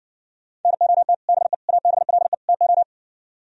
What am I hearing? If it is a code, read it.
Morse code, heard as a keyed tone, about 60 wpm, 700 Hz. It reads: NOT BE ABLE TO